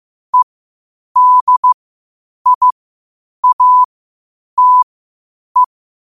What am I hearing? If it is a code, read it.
Morse code, heard as a keyed tone, about 15 wpm, spaced as for 8 wpm, 1000 Hz. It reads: EDIATE
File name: qotdfiles / 8wpm